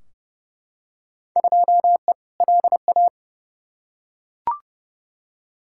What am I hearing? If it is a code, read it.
Morse code, heard as a keyed tone, about 31 wpm, 700 Hz. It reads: 2E LA
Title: Field Day Exchanges Encore - 30wpm